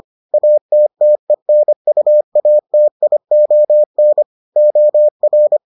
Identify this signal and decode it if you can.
Morse code, heard as a keyed tone, about 25 wpm, 600 Hz. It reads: ATTENUATION OR